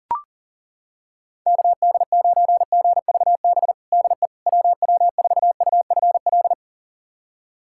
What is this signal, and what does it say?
Morse code, heard as a keyed tone, about 40 wpm, 700 Hz. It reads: KD9GVB DE WW4UFL
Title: USA Callsign DE Callsign - 6 Characters - 40wpm